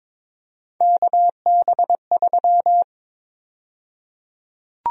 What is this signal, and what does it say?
Morse code, heard as a keyed tone, about 22 wpm, 700 Hz. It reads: KB3
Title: USA Call Sign Prefixes - 22wpm